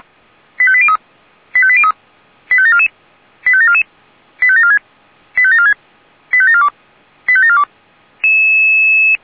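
Five-tone selective call (ZVEI), heard as quick runs of five tones, tones of about 70 ms. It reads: 85803 85803 85744 85744 85746 85746 85742 85742
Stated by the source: Five-tone ZVEI selective call received by SDR.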